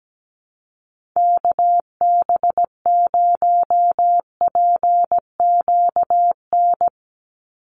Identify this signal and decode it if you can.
Morse code, heard as a keyed tone, about 17 wpm, 700 Hz. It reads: KB0PQN
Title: USA Call Signs 17wpm